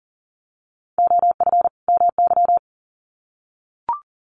Morse code, heard as a keyed tone, about 40 wpm, 700 Hz. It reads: OF MY
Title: Sets of 2 Words 40wpm